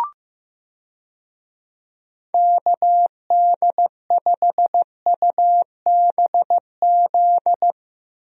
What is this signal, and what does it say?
Morse code, heard as a keyed tone, about 15 wpm, 700 Hz. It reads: KD5UBZ
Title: USA Call Signs - 6 Characters - 15wpm